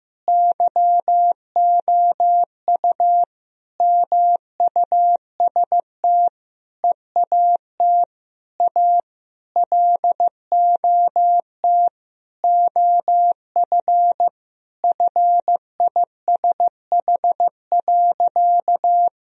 Morse code, heard as a keyed tone, about 15 wpm, 700 Hz. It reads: YOU MUST EAT A LOT OF FISH.
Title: Sentences from Top 1000 Words - Review Part 3 - 15wpm